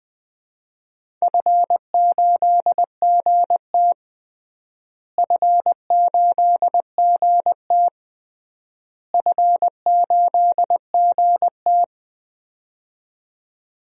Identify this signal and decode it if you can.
Morse code, heard as a keyed tone, about 20 wpm, 700 Hz. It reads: F8GT F8GT F8GT
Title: DXCC-100 Call Signs - 4 Characters - Repeated 3x - 20wpm